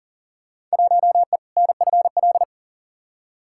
Morse code, heard as a keyed tone, about 40 wpm, 700 Hz. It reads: 1E NFL
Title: Field Day Exchanges - 40wpm